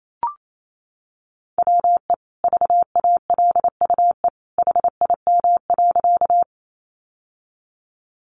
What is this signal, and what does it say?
Morse code, heard as a keyed tone, about 28 wpm, 700 Hz. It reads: WE VALUE HIM.
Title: Sentences from Top 600 Words 28wpm